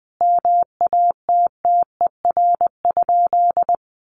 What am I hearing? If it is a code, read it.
Morse code, heard as a keyed tone, about 20 wpm, 700 Hz. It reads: MATTER?
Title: Sentences from Top 1300 Words 20wpm